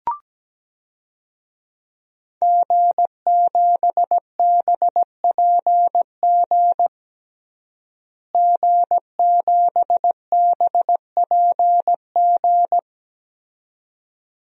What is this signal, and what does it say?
Morse code, heard as a keyed tone, about 17 wpm, 700 Hz. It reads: G7BPG G7BPG
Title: DXCC-100 Call Signs - 5 Characters - Repeated 2x - 17wpm